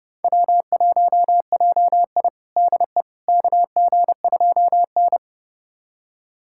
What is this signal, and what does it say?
Morse code, heard as a keyed tone, about 30 wpm, 700 Hz. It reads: W1JI DE KG2N
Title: USA Callsign DE Callsign - 4 Characters - 30wpm